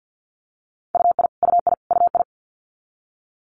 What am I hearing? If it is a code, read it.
Morse code, heard as a keyed tone, about 50 wpm, 690 Hz. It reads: UI UI UI
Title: Multi-letter Phonemes Repeated 3x 50wpm